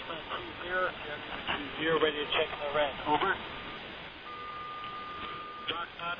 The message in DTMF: none

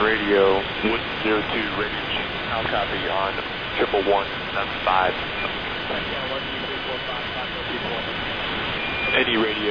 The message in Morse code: none

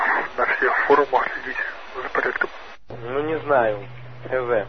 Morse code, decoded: none

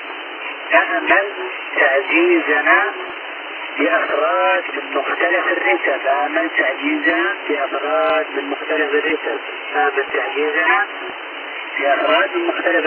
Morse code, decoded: none

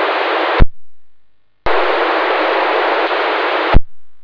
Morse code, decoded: none

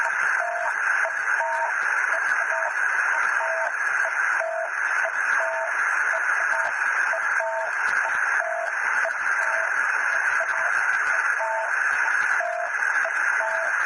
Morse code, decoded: none

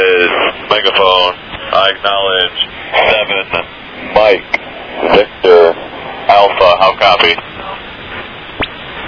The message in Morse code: none

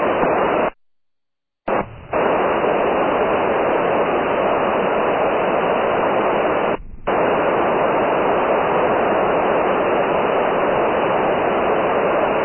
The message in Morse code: none

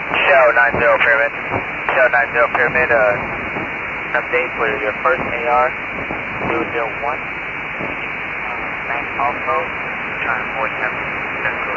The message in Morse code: none

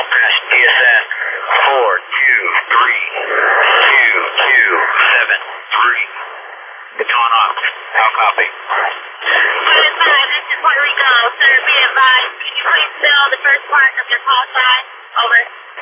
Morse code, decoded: none

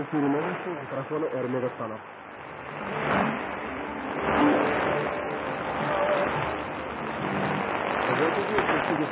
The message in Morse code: none

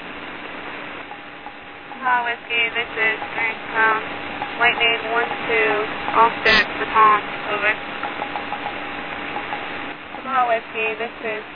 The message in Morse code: none

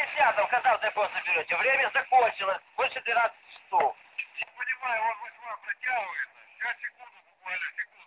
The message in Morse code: none

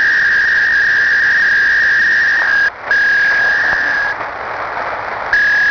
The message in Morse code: none

